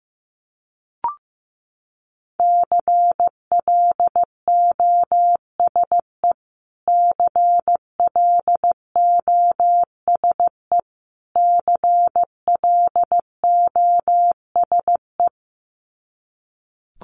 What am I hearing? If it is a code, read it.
Morse code, heard as a keyed tone, about 15 wpm, 700 Hz. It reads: CLOSE CLOSE CLOSE
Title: Top 500 Words Repeated 15wpm